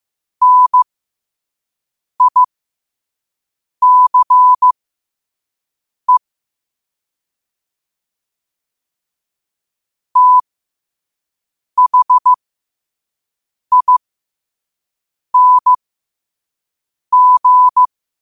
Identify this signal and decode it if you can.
Morse code, heard as a keyed tone, about 15 wpm, spaced as for 5 wpm, 1000 Hz. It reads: NICE THING
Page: Quotes for Wed, 13 Aug 2025 in Morse Code at 5 words per minute.